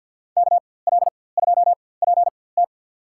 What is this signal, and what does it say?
Morse code, heard as a keyed tone, about 53 wpm, spaced as for 24 wpm, 700 Hz. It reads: KL2PT
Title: USA Call Signs - 5 Characters - 4x Character Spacing - 50wpm